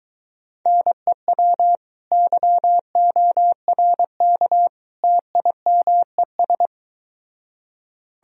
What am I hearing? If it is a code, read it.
Morse code, heard as a keyed tone, about 24 wpm, 700 Hz. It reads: NEW YORK TIMES